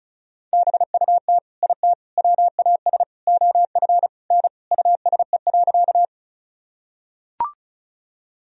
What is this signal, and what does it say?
Morse code, heard as a keyed tone, about 36 wpm, 700 Hz. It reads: BUT IT WAS OF N USE.
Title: Sentences from Top 100 Words and Abbreviations 35wpm